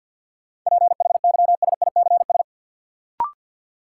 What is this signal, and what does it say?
Morse code, heard as a keyed tone, about 53 wpm, 700 Hz. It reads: PHYSICS
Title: Sentences from Top 1000 Words - Review Part 5 - New Word Drill - 50wpm